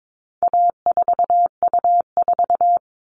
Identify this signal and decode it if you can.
Morse code, heard as a keyed tone, about 22 wpm, 700 Hz. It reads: A4U4
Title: All Cut Numbers - 4-Digit 22wpm